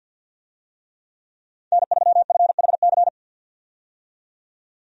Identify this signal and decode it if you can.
Morse code, heard as a keyed tone, about 53 wpm, 700 Hz. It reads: N3FHC
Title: USA Call Signs - 5 Characters - 50wpm